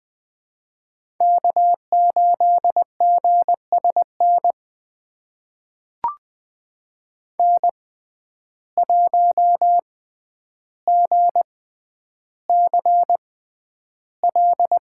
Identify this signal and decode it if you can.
Morse code, heard as a keyed tone, about 20 wpm, spaced as for 10 wpm, 700 Hz. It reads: K8GSN N1GCL
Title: USA Call Signs - 5 Characters - 6x Character Spacing - 20wpm